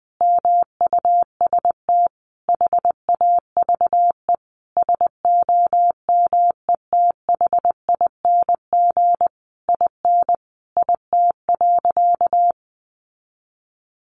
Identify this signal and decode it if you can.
Morse code, heard as a keyed tone, about 20 wpm, 700 Hz. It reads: MUST HAVE SOMETHING IN IT.
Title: Sentences from Top 1100 Words 20wpm